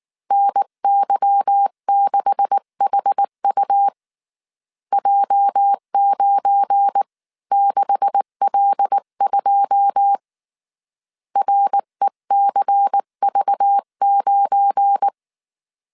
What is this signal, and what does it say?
Morse code, heard as a keyed tone, about 19 wpm, 800 Hz. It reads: NY6HU J96L2 RECV9
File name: CW / DET / 1 9 1500C S B 18wpm 2551